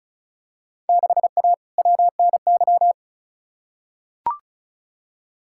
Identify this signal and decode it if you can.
Morse code, heard as a keyed tone, about 36 wpm, 700 Hz. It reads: 6A WNY
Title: Field Day Exchanges Encore - 35wpm